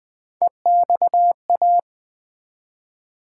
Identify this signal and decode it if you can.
Morse code, heard as a keyed tone, about 20 wpm, 700 Hz. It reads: EXA
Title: Build A Word - 1 Letter At A Time - 5-Letter Words - 20wpm